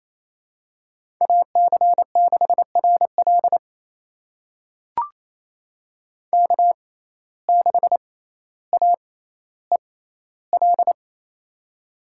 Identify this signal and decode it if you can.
Morse code, heard as a keyed tone, about 28 wpm, spaced as for 14 wpm, 700 Hz. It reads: AC6RL K6AEL
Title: USA Call Signs - 5 Characters - 6x Character Spacing - 28wpm